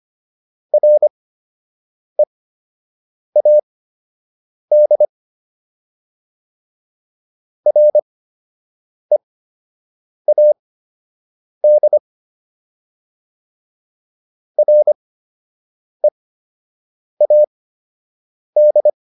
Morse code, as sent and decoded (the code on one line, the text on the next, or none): .-. .- -.. / .-. .- -.. / .-. .- -..
READ READ READ